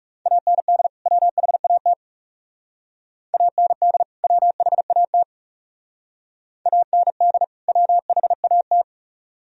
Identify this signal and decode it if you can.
Morse code, heard as a keyed tone, about 36 wpm, 700 Hz. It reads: AND WHAT AND WHAT AND WHAT
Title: Speed-Racing - Sets of 2 Words 30wpm